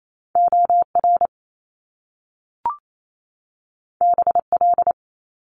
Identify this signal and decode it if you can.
Morse code, heard as a keyed tone, about 28 wpm, 700 Hz. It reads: OR BL
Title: Top Two Letter Combinations in Words 28wpm